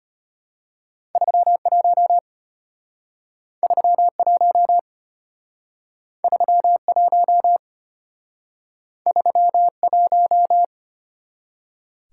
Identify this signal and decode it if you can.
Morse code, heard as a keyed tone, about 34 wpm, 700 Hz. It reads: 31 31 31 31